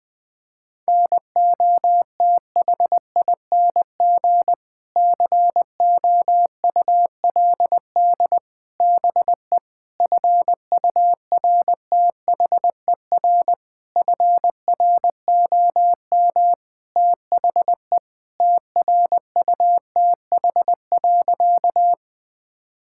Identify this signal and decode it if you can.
Morse code, heard as a keyed tone, about 20 wpm, 700 Hz. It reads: NOTHING COULD BE FURTHER FROM THE TRUTH.
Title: Sentences with 6-Word Phrases - 20wpm